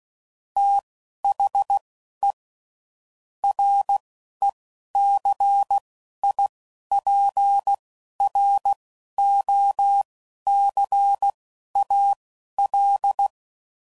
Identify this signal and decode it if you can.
Morse code, heard as a keyed tone, about 16 wpm, spaced as for 12 wpm, 800 Hz. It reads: THE RECIPROCAL
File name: CW Code Groups @10WPM / Technical Subjects / XC